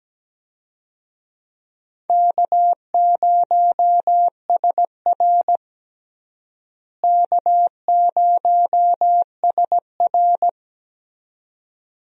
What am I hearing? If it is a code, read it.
Morse code, heard as a keyed tone, about 17 wpm, 700 Hz. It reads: K0SR K0SR
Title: USA Call Signs - 4 Characters - Repeated 2x - 17wpm